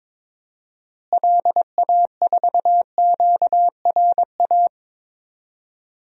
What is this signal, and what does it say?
Morse code, heard as a keyed tone, about 22 wpm, 700 Hz. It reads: LA4QRA
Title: DXCC-100 Call Signs - 6 Characters - Repeated 3x - 22wpm